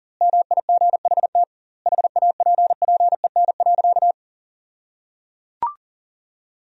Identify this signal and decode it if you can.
Morse code, heard as a keyed tone, about 37 wpm, 700 Hz. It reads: MIGHT HAPPEN.
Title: 2x Word-Spacing - Sentences from Top 400 Words - 40wpm